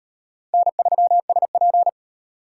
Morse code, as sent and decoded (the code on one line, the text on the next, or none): -. ...-- ... .--.
N3SP